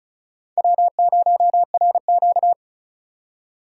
Morse code, heard as a keyed tone, about 34 wpm, 700 Hz. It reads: W0RQ
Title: USA Call Signs - 4 Characters - 4x Character Spacing - 35wpm